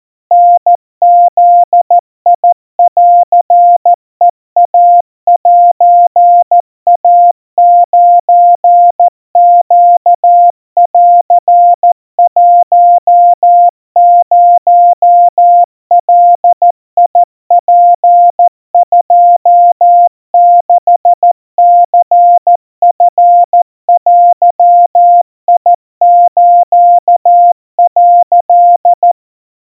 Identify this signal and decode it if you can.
Morse code (Japanese Wabun code), heard as a keyed tone, about 14 wpm, 700 Hz. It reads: タブンヘイセイ9ネン10ガツ26ニチデス」
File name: kawa / Morse sample / kamyu 50 cpm 700 Hz Crispy